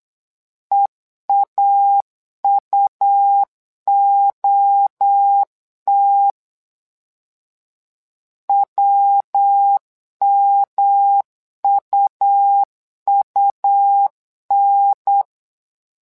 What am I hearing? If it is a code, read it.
Morse code, heard as a keyed tone, about 8 wpm, 800 Hz. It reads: EAUOT WMUUN
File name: LEZIONI / 3 MWRO / 3 MWRO 8wpm 2606